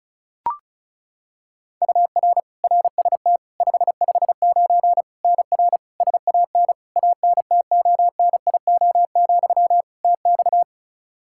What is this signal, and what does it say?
Morse code, heard as a keyed tone, about 36 wpm, 700 Hz. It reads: UR RST 559 NR SAN ANTONIO, TX
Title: K1USN SST QSO Practice 35wpm